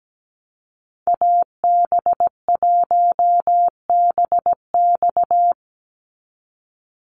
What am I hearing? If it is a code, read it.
Morse code, heard as a keyed tone, about 17 wpm, 700 Hz. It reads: AB1BX
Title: Call Signs Repeated 3x - 17wpm